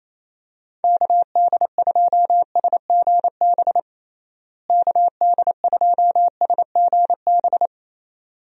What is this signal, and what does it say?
Morse code, heard as a keyed tone, about 28 wpm, 700 Hz. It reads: KD2SGB KD2SGB
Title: USA Call Signs - 6 Characters - Repeated 2x - 28wpm